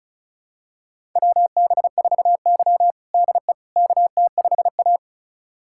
Morse code, heard as a keyed tone, about 36 wpm, 700 Hz. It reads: WB4Y DE KT5A
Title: USA Callsign DE Callsign - 4 Characters - 35wpm